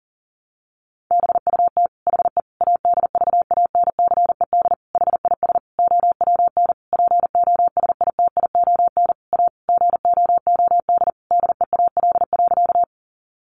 Morse code, heard as a keyed tone, about 40 wpm, 700 Hz. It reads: BUT HE ADVANCED HIS OWN POSITION A GOOD DEAL.
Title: Sentences from Top 1200 Words 40wpm